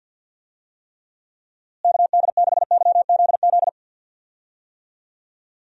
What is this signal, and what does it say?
Morse code, heard as a keyed tone, about 51 wpm, 700 Hz. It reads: KD6YZZ
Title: USA Call Signs - 6 Characters - Repeated 3x - 50wpm